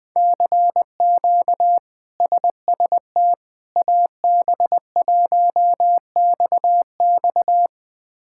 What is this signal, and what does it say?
Morse code, heard as a keyed tone, about 21 wpm, 700 Hz. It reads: CQ SST AB1XX
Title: K1USN SST QSO Practice - 20wpm